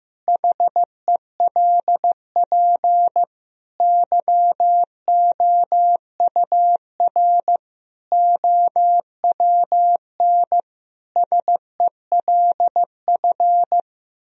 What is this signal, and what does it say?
Morse code, heard as a keyed tone, about 15 wpm, 700 Hz. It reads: HELP YOUR OWN SELF
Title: Sentences from Top 1000 Words - Review Part 1 - 15wpm